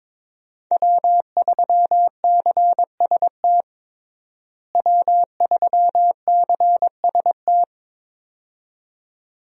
Morse code, heard as a keyed tone, about 22 wpm, 700 Hz. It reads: W3CST W3CST
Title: USA Call Signs - 5 Characters - Repeated 2x - 22wpm